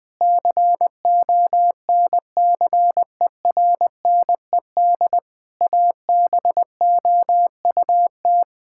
Morse code, heard as a keyed tone, about 21 wpm, 700 Hz. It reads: CONCERNED ABOUT
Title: Sentences from Top 1000 Words - Review Part 6 - 20wpm